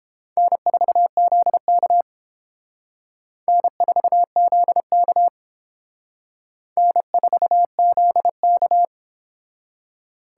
Morse code, heard as a keyed tone, about 31 wpm, 700 Hz. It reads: N4ZK N4ZK N4ZK